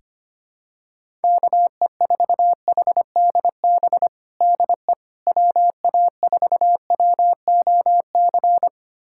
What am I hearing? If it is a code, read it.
Morse code, heard as a keyed tone, about 25 wpm, 700 Hz. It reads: KE4HDB DE WA4WOC
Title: USA Callsign DE Callsign - 6 Characters - 25wpm